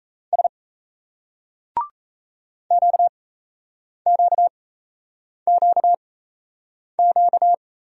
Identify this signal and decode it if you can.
Morse code, heard as a keyed tone, about 38 wpm, spaced as for 8 wpm, 700 Hz. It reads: S QQQQ